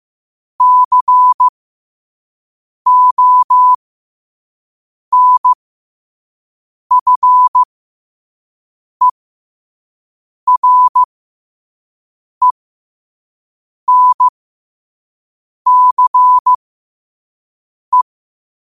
Morse code, heard as a keyed tone, about 15 wpm, spaced as for 5 wpm, 1000 Hz. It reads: CONFERENCE
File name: qotdfiles / 5wpm